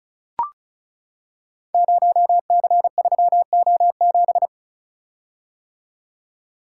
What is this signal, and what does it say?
Morse code, heard as a keyed tone, about 36 wpm, 690 Hz. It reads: 0C3O7
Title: Code Groups - 5 Letter:Numbers - 35wpm